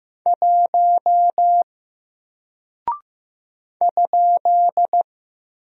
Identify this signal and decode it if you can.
Morse code, heard as a keyed tone, about 15 wpm, 700 Hz. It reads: 1 ?
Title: Single Character (Letter:Number:Punctuation:Prosign) 15wpm